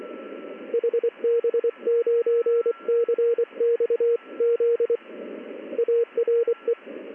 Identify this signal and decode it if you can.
Morse code, heard as a keyed tone, about 24 wpm, 470 Hz. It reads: HB9CXZ ARE
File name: contests / ss-ssb-k5tr- -1700